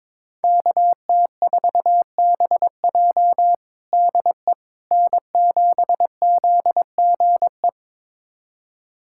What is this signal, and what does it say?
Morse code, heard as a keyed tone, about 22 wpm, 700 Hz. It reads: KT4BJ DE N7ZGE
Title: USA Callsign DE Callsign - 5 Characters - 22wpm